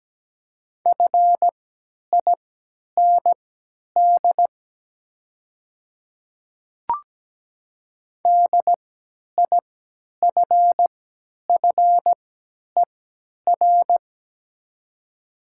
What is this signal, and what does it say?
Morse code, heard as a keyed tone, about 17 wpm, spaced as for 10 wpm, 700 Hz. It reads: FIND DIFFER
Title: 3x Character Spacing - Top 200 Words - 17wpm